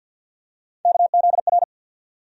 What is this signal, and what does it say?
Morse code, heard as a keyed tone, about 50 wpm, 700 Hz. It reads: KZR
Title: USA Call Sign Postfixes - 3 Letters 50wpm